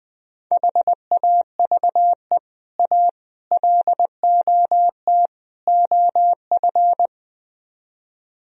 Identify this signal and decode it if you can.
Morse code, heard as a keyed tone, about 20 wpm, 700 Hz. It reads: HAVE A LOT OF